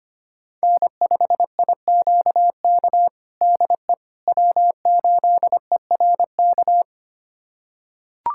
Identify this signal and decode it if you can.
Morse code, heard as a keyed tone, about 25 wpm, 700 Hz. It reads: N5IQK DE W8ERK
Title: USA Callsign DE Callsign - 5 Characters - 25wpm